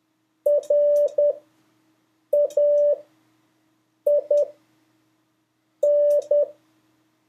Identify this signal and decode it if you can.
Morse code, heard as a keyed tone, about 10 wpm, spaced as for 5 wpm, 580 Hz. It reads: RAIN
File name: 2of5-words-1